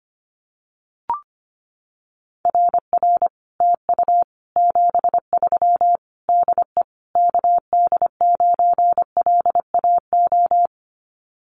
Morse code, heard as a keyed tone, about 25 wpm, 700 Hz. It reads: RR TU 73 DE KD9LAO
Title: Common QSO Phrases Encore 25wpm